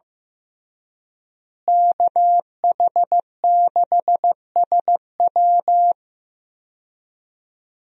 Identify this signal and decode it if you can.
Morse code, heard as a keyed tone, about 15 wpm, 700 Hz. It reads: KH6SW
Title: DXCC-100 Call Signs - 5 Characters - Repeated 2x - 15wpm